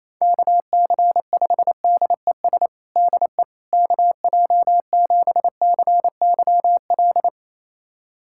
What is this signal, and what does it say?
Morse code, heard as a keyed tone, about 28 wpm, 700 Hz. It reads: KC5DES DE KJ7CYL
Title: USA Callsign DE Callsign - 6 Characters - 28wpm